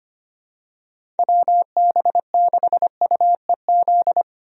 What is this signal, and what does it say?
Morse code, heard as a keyed tone, about 25 wpm, 700 Hz. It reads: WB6UEZ
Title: USA Call Signs - 6 Characters - Repeated 2x - 25wpm